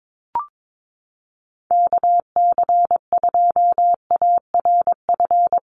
Morse code, heard as a keyed tone, about 22 wpm, 700 Hz. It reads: KC2ARF
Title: USA Call Signs Repeated 2x 22wpm